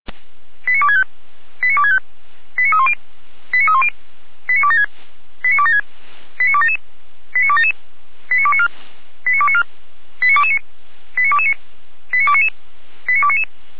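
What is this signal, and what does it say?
Five-tone selective call (ZVEI), heard as quick runs of five tones, tones of about 70 ms. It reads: 89265 89265 89310 89310 89276 89276 89277 89277 89294 89294 89229 89229 89299 89299
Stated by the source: ZVEI five-tone selective call received by SDR, from the Feuerwehr Orlamünde fire brigade.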